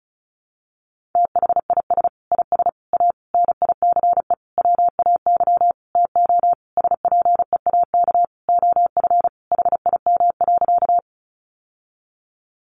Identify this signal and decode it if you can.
Morse code, heard as a keyed tone, about 34 wpm, 700 Hz. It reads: THIS IS A NICE WAY TO SPEAK OF HIM.
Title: Sentences from Top 1100 Words 35wpm